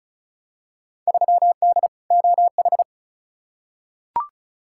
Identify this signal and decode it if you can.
Morse code, heard as a keyed tone, about 36 wpm, 700 Hz. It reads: 3D OH
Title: Field Day Exchanges - 35wpm